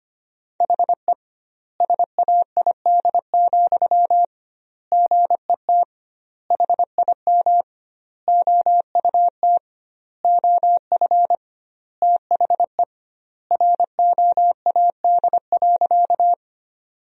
Morse code, heard as a keyed tone, about 25 wpm, 700 Hz. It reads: HE SAID, GET HIM OUT OF THE ROAD.
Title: 2x Word-Spacing - Sentences from Top 400 Words - 25wpm